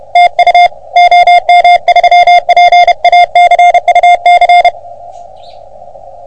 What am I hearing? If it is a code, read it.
Morse code, heard as a keyed tone, about 30 wpm, 720 Hz. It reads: TU OM3PA CUC
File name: OM3PA PILUP CUC 388